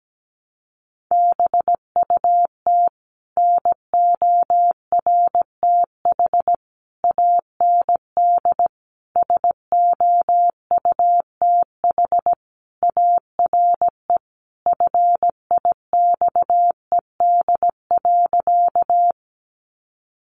Morse code, heard as a keyed tone, about 17 wpm, 700 Hz. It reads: BUT NORTH AND SOUTH ARE FIXED.
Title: Sentences from Top 1100 Words 17wpm